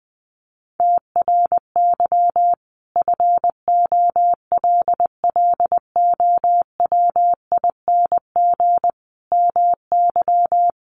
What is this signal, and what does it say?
Morse code, heard as a keyed tone, about 20 wpm, 700 Hz. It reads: TRY FOLLOWING MY
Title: Sentences from Top 1100 Words 20wpm